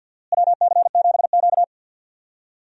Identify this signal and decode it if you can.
Morse code, heard as a keyed tone, about 53 wpm, 690 Hz. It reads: WC7Q
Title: USA Call Signs - 4 Characters - Repeated 2x - 50wpm